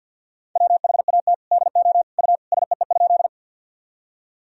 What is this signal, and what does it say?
Morse code, heard as a keyed tone, about 51 wpm, 700 Hz. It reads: WHAT DO U SEE?
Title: Sentences from Top 100 Words and Abbreviations 50wpm